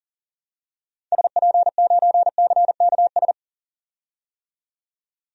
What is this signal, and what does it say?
Morse code, heard as a keyed tone, about 38 wpm, 700 Hz. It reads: SP9CKS
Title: DXCC-100 Call Signs - 6 Characters - 40wpm